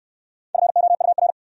Morse code, heard as a keyed tone, about 61 wpm, 700 Hz. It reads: FOUR